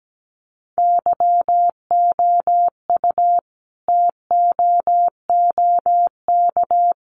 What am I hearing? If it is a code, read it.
Morse code, heard as a keyed tone, about 17 wpm, 700 Hz. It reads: YOU TOOK
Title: Sentences from Top 300 Words 17wpm